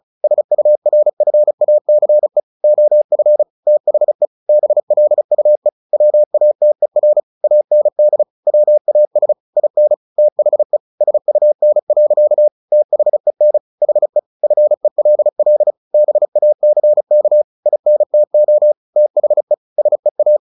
Morse code, heard as a keyed tone, about 34 wpm, 600 Hz. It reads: SURFACE OF THE BLUE WATER AND WAS IN THE SUN. THEN HE FELL BACK INTO THE SEA
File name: the old man and the sea 2 35wpm